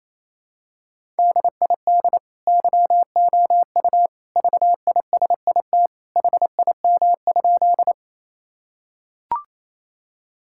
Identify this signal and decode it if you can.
Morse code, heard as a keyed tone, about 28 wpm, 700 Hz. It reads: DID YOU VISIT HIM?
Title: Sentences from Top 1000 Words - Review Part 6 - 28wpm